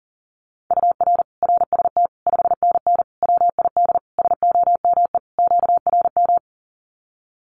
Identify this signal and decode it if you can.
Morse code, heard as a keyed tone, about 40 wpm, 700 Hz. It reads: UR RST 5NN WID SOME QRM
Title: Common QSO Phrases 40wpm